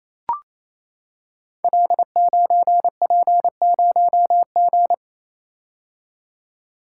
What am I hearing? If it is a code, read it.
Morse code, heard as a keyed tone, about 27 wpm, 700 Hz. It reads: L9P0G
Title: Code Groups - 5 Letter:Numbers - 28wpm